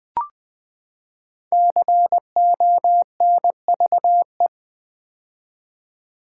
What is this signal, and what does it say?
Morse code, heard as a keyed tone, about 20 wpm, 700 Hz. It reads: CONVE
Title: Build A Word - 5 Letters At A Time - 10-Letter Words - 20wpm